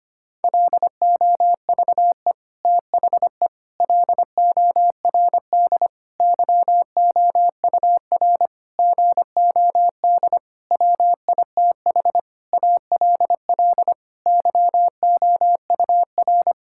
Morse code, heard as a keyed tone, about 25 wpm, 700 Hz. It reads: LOVE THE LORD YOUR GOD WITH ALL YOUR
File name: Sets of 9 Words - 25wpm